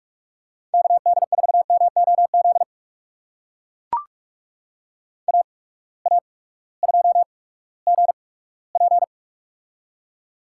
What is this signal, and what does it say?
Morse code, heard as a keyed tone, about 44 wpm, spaced as for 20 wpm, 690 Hz. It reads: KD4MOZ AA2GP
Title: USA Call Signs - 8x Character Spacing - 45wpm